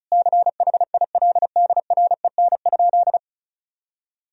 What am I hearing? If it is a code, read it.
Morse code, heard as a keyed tone, about 34 wpm, 700 Hz. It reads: CHILDREN?
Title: Sentences from Top 1000 Words - Review Part 2 - 35wpm